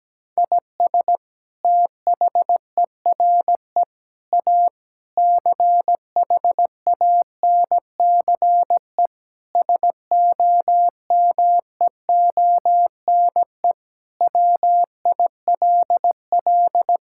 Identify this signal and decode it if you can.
Morse code, heard as a keyed tone, about 17 wpm, 700 Hz. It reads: IS THERE A CHANCE SOMEONE WILL
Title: Sentences from Top 1000 Words - Review Part 6 - 17wpm